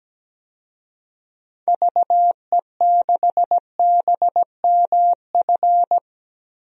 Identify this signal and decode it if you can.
Morse code, heard as a keyed tone, about 17 wpm, 700 Hz. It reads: VE6BMF
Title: DXCC-100 Call Signs - 6 Characters - 17wpm